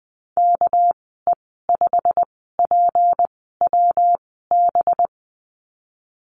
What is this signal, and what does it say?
Morse code, heard as a keyed tone, about 20 wpm, 700 Hz. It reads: K E 5 P W B
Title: 200 Lesson 32g - Letter k - Call Signs 20wpm